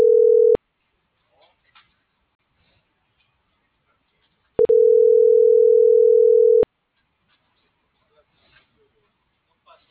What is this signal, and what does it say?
Ringback tone (North America), 440 + 480 Hz.